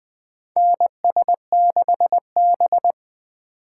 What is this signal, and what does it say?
Morse code, heard as a keyed tone, about 20 wpm, 700 Hz. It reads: NS6B
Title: USA Call Signs - 4 Characters - 8x Character Spacing - 20wpm